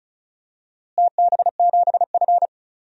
Morse code, heard as a keyed tone, about 34 wpm, 700 Hz. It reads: TB7F
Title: DXCC-100 Call Signs - 4 Characters - Repeated 2x - 35wpm